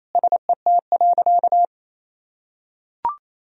Morse code, heard as a keyed tone, about 27 wpm, 700 Hz. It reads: SET.
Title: Sentences from Top 400 Words and Abbreviations 28wpm